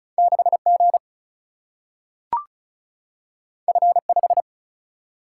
Morse code, heard as a keyed tone, about 36 wpm, 700 Hz. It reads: 6 G F 5
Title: Single-Letter Single-Number 35wpm